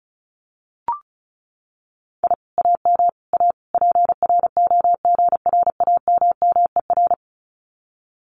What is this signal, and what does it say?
Morse code, heard as a keyed tone, about 36 wpm, 700 Hz. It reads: I AM A PROGRAMMER
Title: Common QSO Phrases Encore 35wpm